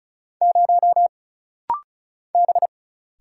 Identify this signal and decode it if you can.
Morse code, heard as a keyed tone, about 36 wpm, 700 Hz. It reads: 0 B
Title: Single Letter-Number Alt - Rapid-Fire 35wpm